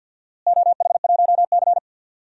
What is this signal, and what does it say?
Morse code, heard as a keyed tone, about 50 wpm, 700 Hz. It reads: OH1C
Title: DXCC-100 Call Signs - 4 Characters - Repeated 3x - 50wpm